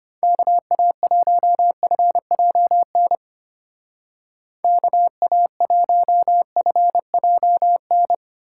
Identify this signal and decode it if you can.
Morse code, heard as a keyed tone, about 32 wpm, 700 Hz. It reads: KA1FJN KA1FJN